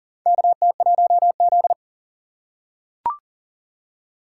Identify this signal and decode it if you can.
Morse code, heard as a keyed tone, about 39 wpm, 700 Hz. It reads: KT1Z
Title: USA Call Signs - 4 Characters - 2x Character Spacing - 40wpm